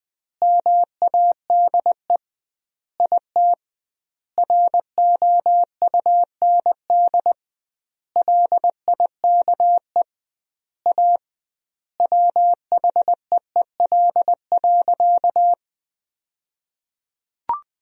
Morse code, heard as a keyed tone, about 21 wpm, 700 Hz. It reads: MADE IT ROUND LIKE A WHEEL.
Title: 2x Word-Spacing - Sentences from Top 500 Words - 20wpm